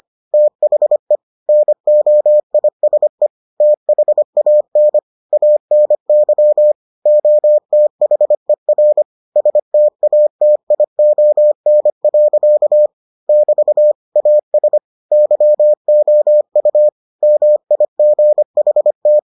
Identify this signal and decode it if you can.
Morse code, heard as a keyed tone, about 25 wpm, 600 Hz. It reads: THE NOISE THAN ANY OTHER STATION. <BT> AS YOU MIGHT